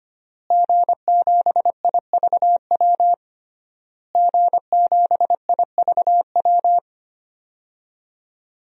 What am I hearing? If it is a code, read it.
Morse code, heard as a keyed tone, about 25 wpm, 700 Hz. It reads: G7IVW G7IVW
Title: DXCC-100 Call Signs - 5 Characters - Repeated 3x - 25wpm